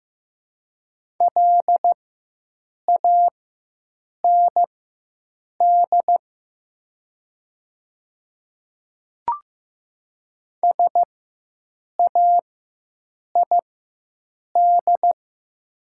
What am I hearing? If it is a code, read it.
Morse code, heard as a keyed tone, about 15 wpm, spaced as for 7 wpm, 700 Hz. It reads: LAND SAID
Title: 4x Character Spacing - Top 300 Words - 15wpm